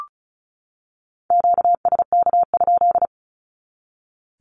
Morse code, heard as a keyed tone, about 34 wpm, 700 Hz. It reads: QSK?
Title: Q-Codes 35wpm